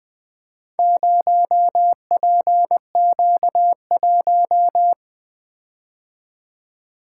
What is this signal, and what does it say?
Morse code, heard as a keyed tone, about 20 wpm, 700 Hz. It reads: 0PQ1
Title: Four Letter-Number Combinations - 20wpm